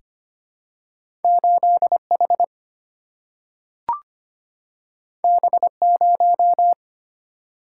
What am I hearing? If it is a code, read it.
Morse code, heard as a keyed tone, about 25 wpm, 700 Hz. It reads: 8 H B 0
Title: Single-Letter Single-Number 25wpm